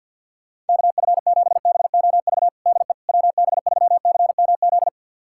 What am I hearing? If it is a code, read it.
Morse code, heard as a keyed tone, about 52 wpm, 700 Hz. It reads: KF7BOV DE WB3CMZ
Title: USA Callsign DE Callsign 50wpm